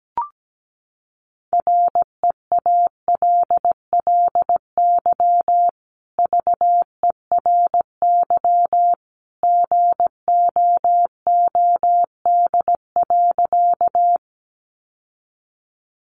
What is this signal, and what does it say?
Morse code, heard as a keyed tone, about 17 wpm, 700 Hz. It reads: REALLY VERY GOOD.
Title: Sentences from Top 600 Words 17wpm